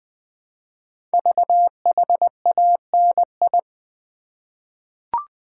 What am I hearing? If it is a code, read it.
Morse code, heard as a keyed tone, about 20 wpm, 700 Hz. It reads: VHANI
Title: Code Groups - 5 Letters - 20wpm